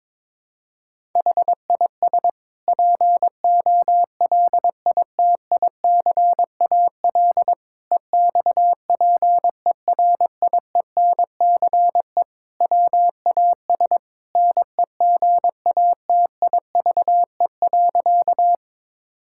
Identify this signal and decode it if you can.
Morse code, heard as a keyed tone, about 22 wpm, 700 Hz. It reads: HIS POLITICAL EXPERIENCE WAS NEGATIVE.
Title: Sentences from Top 1000 Words - Review Part 6 - 22wpm